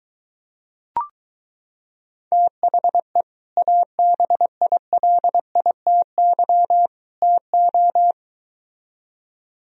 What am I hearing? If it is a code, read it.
Morse code, heard as a keyed tone, about 23 wpm, 700 Hz. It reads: THE ABILITY TO